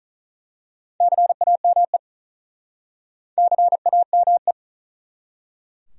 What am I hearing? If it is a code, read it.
Morse code, heard as a keyed tone, about 45 wpm, 700 Hz. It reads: CAME CAME